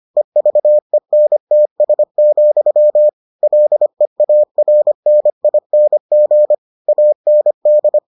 Morse code, heard as a keyed tone, about 25 wpm, 600 Hz. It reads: EVENTS, LEARNING AND